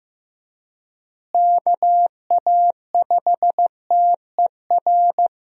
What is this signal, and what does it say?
Morse code, heard as a keyed tone, about 15 wpm, 700 Hz. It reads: KA5TER
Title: USA Call Signs - 6 Characters - Repeated 3x - 15wpm